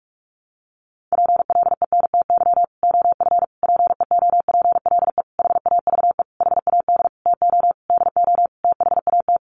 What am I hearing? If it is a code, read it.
Morse code, heard as a keyed tone, about 44 wpm, 700 Hz. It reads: PLENTY OF PEOPLE HAVE HAD TO DO THAT
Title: Sentences from Top 1300 Words 45wpm